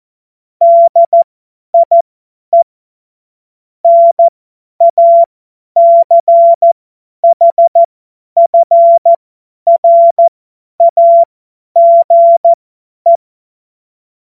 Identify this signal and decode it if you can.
Morse code, heard as a keyed tone, about 14 wpm, spaced as for 10 wpm, 680 Hz. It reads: DIE NACHFRAGE
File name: Morsekurs / Witze / Witz 30. Dezember 2025 50 F70